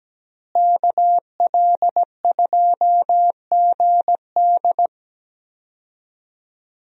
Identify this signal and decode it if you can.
Morse code, heard as a keyed tone, about 17 wpm, 700 Hz. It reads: KL2GD
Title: USA Call Signs - 5 Characters - Repeated 2x - 17wpm